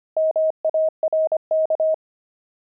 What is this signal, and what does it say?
Morse code, heard as a keyed tone, about 25 wpm, 630 Hz. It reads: MARK
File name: pangram-fs-25-cs-25-javelins